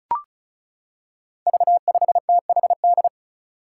Morse code, heard as a keyed tone, about 34 wpm, 700 Hz. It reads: V5THD
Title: Five Challenging Letter-Number Combinations 35wpm